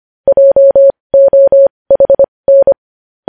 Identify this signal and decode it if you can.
Morse code, heard as a keyed tone, about 25 wpm, spaced as for 20 wpm, 550 Hz. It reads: JOHN
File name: podcast / News headlines-1x-25x20